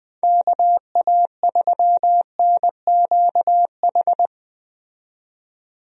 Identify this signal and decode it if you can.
Morse code, heard as a keyed tone, about 20 wpm, 700 Hz. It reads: KA3NQH
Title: USA Call Signs - 6 Characters - 20wpm